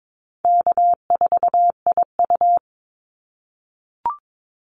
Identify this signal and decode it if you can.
Morse code, heard as a keyed tone, about 22 wpm, 700 Hz. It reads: K4IU
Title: Easy Call Signs - 22wpm